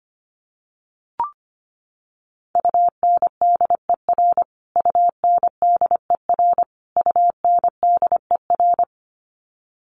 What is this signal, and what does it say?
Morse code, heard as a keyed tone, about 25 wpm, 700 Hz. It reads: UNDER UNDER UNDER
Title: Top 500 Words Repeated 25wpm